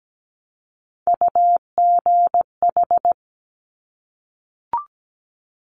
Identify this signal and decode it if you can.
Morse code, heard as a keyed tone, about 17 wpm, 700 Hz. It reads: UGH
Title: Top Three Letter Combinations in Words 17wpm